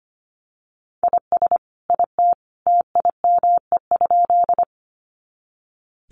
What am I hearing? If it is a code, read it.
Morse code, heard as a keyed tone, about 25 wpm, 700 Hz. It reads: IS IT TIME?
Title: Sentences from Top 100 Words 25wpm